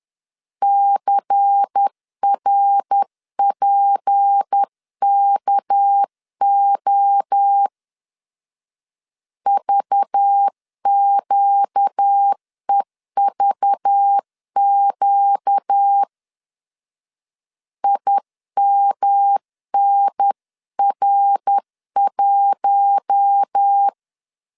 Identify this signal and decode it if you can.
Morse code, heard as a keyed tone, about 10 wpm, 800 Hz. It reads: CRPKO VQEVQ IMNR1
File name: CW / DET / 1 6 1500C S C 10wpm 4252